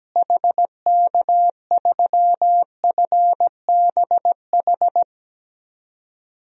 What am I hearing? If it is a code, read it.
Morse code, heard as a keyed tone, about 17 wpm, 700 Hz. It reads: HK3FBH
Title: DXCC-100 Call Signs - 6 Characters - Repeated 2x - 17wpm